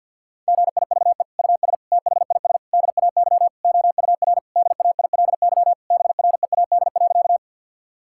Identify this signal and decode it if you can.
Morse code, heard as a keyed tone, about 52 wpm, 700 Hz. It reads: GIVE US THIS DAY OUR DAILY BREAD.
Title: Sentences from Top 1600 Words 50wpm